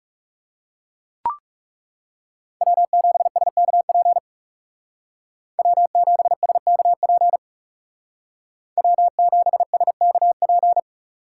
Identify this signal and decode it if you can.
Morse code, heard as a keyed tone, about 36 wpm, 700 Hz. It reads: W7SKP W7SKP W7SKP
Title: Speed Racing - USA Call Signs 30wpm